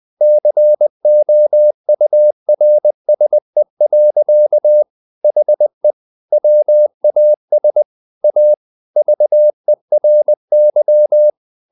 Morse code, heard as a keyed tone, about 20 wpm, 600 Hz. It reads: COURSE. HE WAS A VERY
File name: the old man and the sea part 2 20wpm